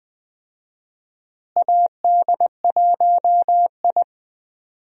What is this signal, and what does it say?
Morse code, heard as a keyed tone, about 21 wpm, 700 Hz. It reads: AD1I
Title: USA Call Signs - 4 Characters - Repeated 2x - 20wpm